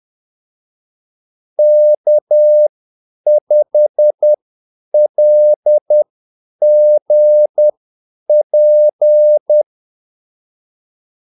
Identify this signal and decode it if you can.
Morse code, heard as a keyed tone, about 10 wpm, spaced as for 8 wpm, 600 Hz. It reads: K5LGP
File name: Callsigns-10-8wpm